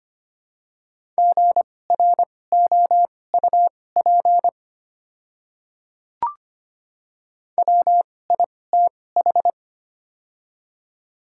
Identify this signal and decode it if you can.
Morse code, heard as a keyed tone, about 25 wpm, spaced as for 18 wpm, 700 Hz. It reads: GROUP WITH
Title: 2x Character Spacing - Top 300 Words - 25wpm